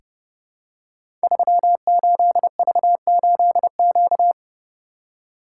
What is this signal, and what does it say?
Morse code, heard as a keyed tone, about 30 wpm, 700 Hz. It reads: 38V8Q
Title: Code Groups - 5 Letter:Numbers - 30wpm